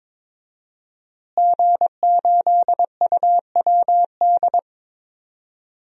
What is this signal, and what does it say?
Morse code, heard as a keyed tone, about 22 wpm, 700 Hz. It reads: G8UWD
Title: DXCC-100 Call Signs - 5 Characters - Repeated 2x - 22wpm